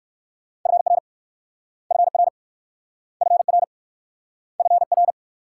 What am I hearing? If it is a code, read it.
Morse code, heard as a keyed tone, about 61 wpm, 700 Hz. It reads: FR FR FR FR